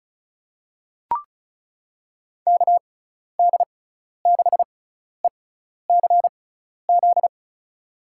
Morse code, heard as a keyed tone, about 36 wpm, spaced as for 12 wpm, 690 Hz. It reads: KD6ECZ
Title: USA Call Signs - 6 Characters - 6x Character Spacing - 35wpm